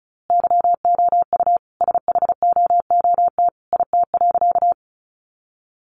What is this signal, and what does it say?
Morse code, heard as a keyed tone, about 34 wpm, 700 Hz. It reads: YOU SHOOT IT.
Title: A Sound Of Thunder 35wpm